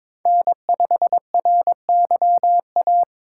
Morse code, heard as a keyed tone, about 22 wpm, 700 Hz. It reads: N5RYA
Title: USA Call Signs - 5 Characters - 2x Character Spacing - 22wpm